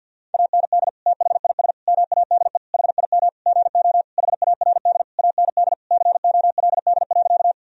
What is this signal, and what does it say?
Morse code, heard as a keyed tone, about 51 wpm, 700 Hz. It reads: AND THIS MADE HIM GO HARD AND COLD.
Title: Sentences from Top 1000 Words - Review Part 1 - 50wpm